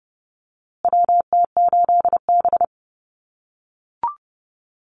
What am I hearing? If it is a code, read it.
Morse code, heard as a keyed tone, about 30 wpm, 700 Hz. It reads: WT8B
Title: USA Call Signs Repeated 3x 30wpm